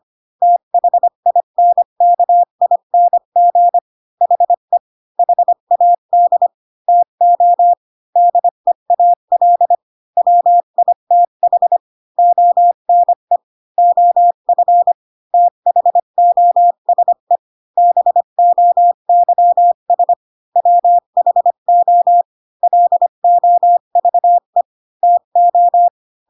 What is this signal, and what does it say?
Morse code, heard as a keyed tone, about 25 wpm, 700 Hz. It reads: THINKING HE HAD TO DEAL WITH ONE OF THOSE BOYS WHO LOVE TO